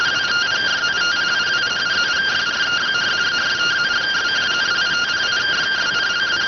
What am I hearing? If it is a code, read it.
Two-tone teleprinter signal, about 170 Hz shift.